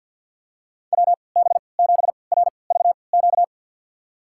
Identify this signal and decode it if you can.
Morse code, heard as a keyed tone, about 46 wpm, spaced as for 27 wpm, 700 Hz. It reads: WB7RVQ
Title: USA Call Signs - 6 Characters - 3x Character Spacing - 50wpm